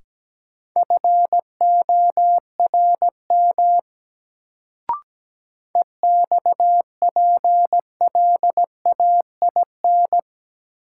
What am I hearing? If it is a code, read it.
Morse code, heard as a keyed tone, about 17 wpm, 700 Hz. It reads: FORM EXPLAIN
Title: Sentences from Top 1100 Words - New Word Drill Alt - 17wpm